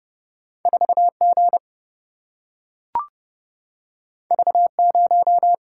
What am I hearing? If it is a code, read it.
Morse code, heard as a keyed tone, about 30 wpm, 700 Hz. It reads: 4 G V 0
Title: Single-Letter Single-Number 30wpm